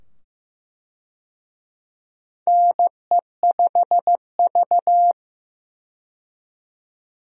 Morse code, heard as a keyed tone, about 15 wpm, 700 Hz. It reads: NE5V
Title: USA Call Signs - 4 Characters - 15wpm